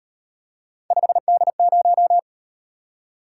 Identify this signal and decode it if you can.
Morse code, heard as a keyed tone, about 36 wpm, 700 Hz. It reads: 5D0